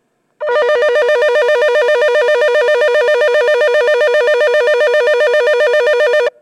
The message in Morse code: none